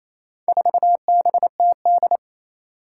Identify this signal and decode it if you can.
Morse code, heard as a keyed tone, about 28 wpm, 700 Hz. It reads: 4BTD
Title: Four Challenging Letter-Number Combinations 28wpm